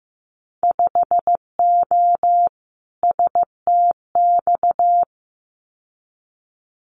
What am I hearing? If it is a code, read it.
Morse code, heard as a keyed tone, about 15 wpm, 700 Hz. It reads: 5O STX
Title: Winter Field Day Exchanges 15wpm